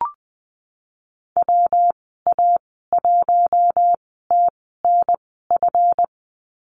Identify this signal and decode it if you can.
Morse code, heard as a keyed tone, about 19 wpm, 700 Hz. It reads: W A 1 T N F
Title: 238 Lesson 38g - Letter q - Call Signs 20wpm